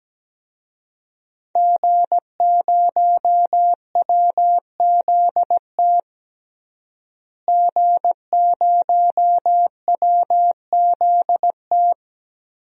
Morse code, heard as a keyed tone, about 17 wpm, 700 Hz. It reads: G0WZT G0WZT
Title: DXCC-100 Call Signs - 5 Characters - Repeated 2x - 17wpm